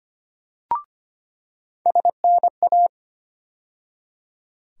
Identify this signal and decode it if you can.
Morse code, heard as a keyed tone, about 25 wpm, 700 Hz. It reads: SNA
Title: USA Call Sign Postfixes - 3 Letters 25wpm